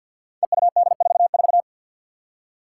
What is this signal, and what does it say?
Morse code, heard as a keyed tone, about 52 wpm, 700 Hz. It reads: EUD44
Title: Five Challenging Letter-Number Combinations 50wpm